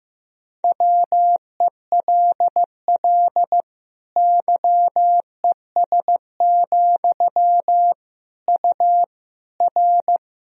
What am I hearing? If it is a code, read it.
Morse code, heard as a keyed tone, about 15 wpm, 700 Hz. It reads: WELL YES, U R
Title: Sentences from Top 500 Words and Abbreviations 15wpm